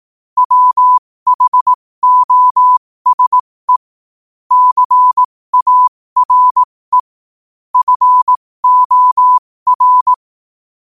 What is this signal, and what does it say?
Morse code, heard as a keyed tone, about 18 wpm, spaced as for 15 wpm, 1000 Hz. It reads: WHOSE CARE FOR
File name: qotdfiles / 15wpm